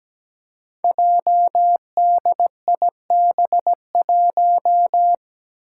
Morse code, heard as a keyed tone, about 17 wpm, 700 Hz. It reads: JDIB1
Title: Five Challenging Letter-Number Combinations 17wpm